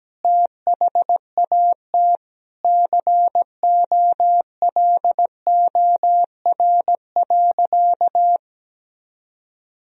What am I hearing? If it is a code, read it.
Morse code, heard as a keyed tone, about 17 wpm, 700 Hz. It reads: THAT COLOR.
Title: Sentences from Top 1000 Words - Review Part 2 - 17wpm